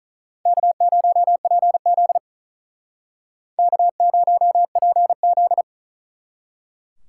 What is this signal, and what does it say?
Morse code, heard as a keyed tone, about 46 wpm, 700 Hz. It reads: K0PZ K0PZ